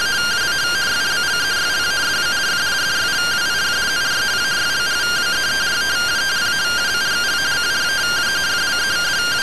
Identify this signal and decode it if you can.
Two-tone teleprinter signal, about 170 Hz shift.